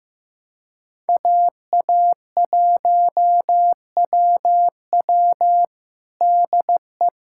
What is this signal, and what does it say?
Morse code, heard as a keyed tone, about 15 wpm, 700 Hz. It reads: AA1WW DE
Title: USA Callsign DE Callsign - 5 Characters - 15wpm